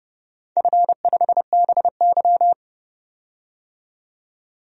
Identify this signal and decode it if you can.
Morse code, heard as a keyed tone, about 30 wpm, 700 Hz. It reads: F5BY
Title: DXCC-100 Call Signs - 4 Characters - 30wpm